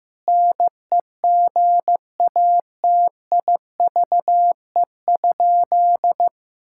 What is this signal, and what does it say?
Morse code, heard as a keyed tone, about 16 wpm, 700 Hz. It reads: NEGATIVE?
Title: Sentences from Top 1000 Words - Review Part 6 - 15wpm